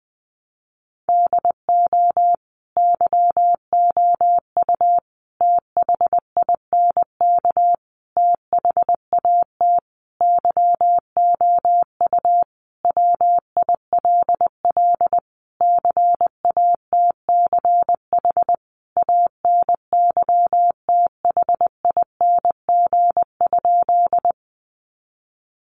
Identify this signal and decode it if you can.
Morse code, heard as a keyed tone, about 20 wpm, 700 Hz. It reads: DO YOU THINK THAT YOU WILL CATCH ANYTHING?
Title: Sentences from Top 1500 Words 20wpm